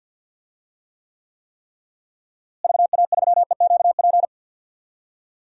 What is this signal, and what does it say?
Morse code, heard as a keyed tone, about 51 wpm, 700 Hz. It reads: VA3EQP
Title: DXCC-100 Call Signs - 6 Characters - Repeated 3x - 50wpm